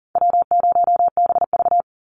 Morse code, heard as a keyed tone, about 40 wpm, 700 Hz. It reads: W0BV
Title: Easy Call Signs Repeated 2x - 40wpm